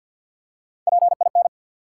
Morse code, heard as a keyed tone, about 48 wpm, 700 Hz. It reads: PIN
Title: Three Letter Words - 50wpm